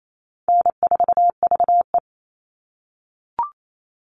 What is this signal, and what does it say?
Morse code, heard as a keyed tone, about 28 wpm, 700 Hz. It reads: N4VE
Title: All Cut Numbers - 4-Digit 28wpm